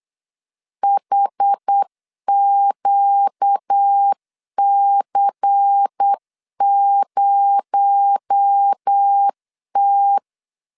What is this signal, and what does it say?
Morse code, heard as a keyed tone, about 8 wpm, 800 Hz. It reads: HQC0T
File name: CW / DET / 1 9 1500C S A 8wpm 5637